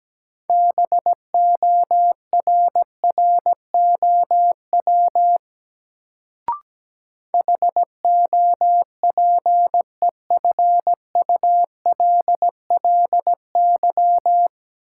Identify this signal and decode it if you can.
Morse code, heard as a keyed tone, about 17 wpm, 700 Hz. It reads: BORROW HOPEFULLY
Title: Sentences from Top 1500 Words - New Word Drill - 17wpm